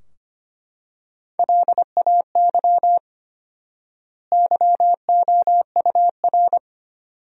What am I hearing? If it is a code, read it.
Morse code, heard as a keyed tone, about 25 wpm, 700 Hz. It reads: LAY YOUR
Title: 4x Word-Spacing - Sentences from Top 400 Words - 25wpm